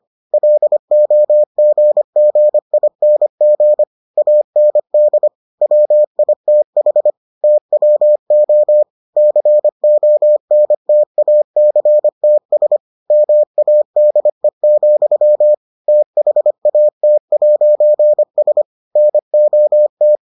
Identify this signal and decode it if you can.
Morse code, heard as a keyed tone, about 25 wpm, 600 Hz. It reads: LOGGING AND WITH TWO CONTACTS MADE, THAT'S NOT